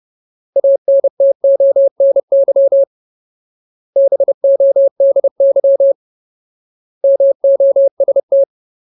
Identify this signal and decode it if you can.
Morse code, heard as a keyed tone, about 30 wpm, 550 Hz. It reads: ANTONY BODY MOST
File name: Ultimate Word List / 30WPM Ultimate